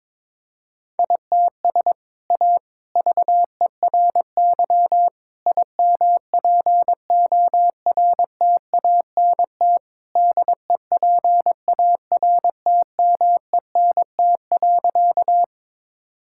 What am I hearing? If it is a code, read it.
Morse code, heard as a keyed tone, about 21 wpm, 700 Hz. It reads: ITS A VERY IMPORTANT DEPARTMENT.
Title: Sentences from Top 1000 Words - Review Part 5 - 22wpm